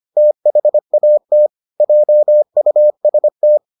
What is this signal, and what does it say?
Morse code, heard as a keyed tone, about 25 wpm, 600 Hz. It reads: THAT JUST